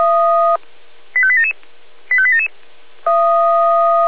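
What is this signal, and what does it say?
Five-tone selective call (ZVEI), heard as quick runs of five tones, tones of about 70 ms. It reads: 85877 85877
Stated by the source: Received on an SDR, ZVEI five-tone selective call.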